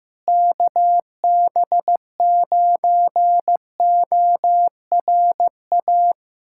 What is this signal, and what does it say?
Morse code, heard as a keyed tone, about 15 wpm, 700 Hz. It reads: KB9ORA
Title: USA Callsign DE Callsign 15wpm